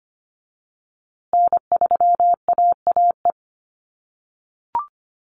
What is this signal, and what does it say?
Morse code, heard as a keyed tone, about 25 wpm, 700 Hz. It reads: N3AAE
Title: USA Call Signs Repeated 3x 25wpm